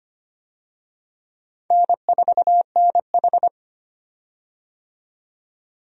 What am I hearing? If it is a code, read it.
Morse code, heard as a keyed tone, about 25 wpm, 700 Hz. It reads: N4NH
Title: USA Call Signs - 4 Characters - 25wpm